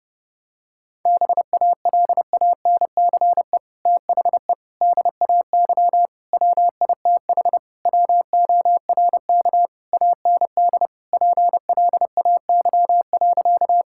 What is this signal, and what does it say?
Morse code, heard as a keyed tone, about 30 wpm, 700 Hz. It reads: BALANCE THE DAY WITH WORK AND PLAY.
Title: Sentences from Top 1000 Words - Review Part 6 - 30wpm